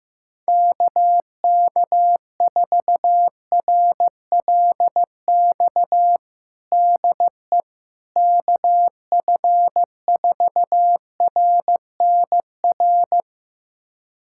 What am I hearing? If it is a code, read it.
Morse code, heard as a keyed tone, about 15 wpm, 700 Hz. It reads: KK4RLX DE KF4RNR
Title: USA Callsign DE Callsign - 6 Characters - 15wpm